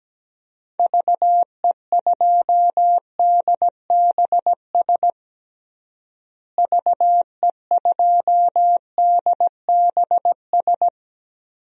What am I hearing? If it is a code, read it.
Morse code, heard as a keyed tone, about 17 wpm, 700 Hz. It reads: VE2DBS VE2DBS
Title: DXCC-100 Call Signs - 6 Characters - Repeated 2x - 17wpm